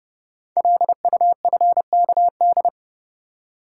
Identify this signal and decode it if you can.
Morse code, heard as a keyed tone, about 29 wpm, 690 Hz. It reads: LUFKD
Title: Code Groups - 5 Letters - 30wpm